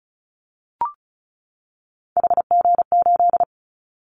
Morse code, heard as a keyed tone, about 34 wpm, 700 Hz. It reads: HG8
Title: Callsign Prefixes for Top 25 DXCC - 35wpm